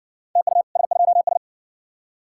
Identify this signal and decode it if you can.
Morse code, heard as a keyed tone, about 54 wpm, 700 Hz. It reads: TU S2S